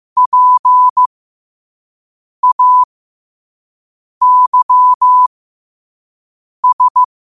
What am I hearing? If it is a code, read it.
Morse code, heard as a keyed tone, about 15 wpm, spaced as for 5 wpm, 1000 Hz. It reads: PAYS